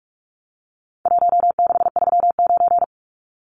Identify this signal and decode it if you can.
Morse code, heard as a keyed tone, about 44 wpm, 700 Hz. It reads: 1639
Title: Numbers - Contest Style - 4-Digit 45wpm